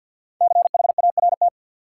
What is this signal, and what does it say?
Morse code, heard as a keyed tone, about 53 wpm, 700 Hz. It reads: CHART
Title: Sentences from Top 1300 Words - New Word Drill - 50wpm